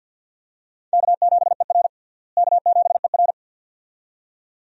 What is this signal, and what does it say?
Morse code, heard as a keyed tone, about 53 wpm, 700 Hz. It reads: K7ER K7ER
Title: USA Call Signs - 4 Characters - Repeated 2x - 50wpm